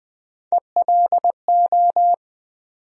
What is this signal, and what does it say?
Morse code, heard as a keyed tone, about 20 wpm, 700 Hz. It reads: ELO
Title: USA Call Sign Postfixes - 3 Letters 20wpm